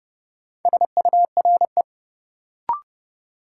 Morse code, heard as a keyed tone, about 30 wpm, 700 Hz. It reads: SURE
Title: Sentences from Top 300 Words - New Word Drill - 30wpm